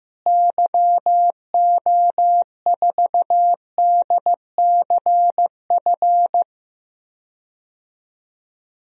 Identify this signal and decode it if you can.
Morse code, heard as a keyed tone, about 16 wpm, 700 Hz. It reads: YO4DCF
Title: DXCC-100 Call Signs - 6 Characters - Repeated 2x - 15wpm